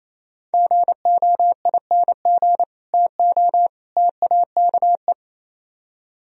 Morse code, heard as a keyed tone, about 37 wpm, 700 Hz. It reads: GOING TO TAKE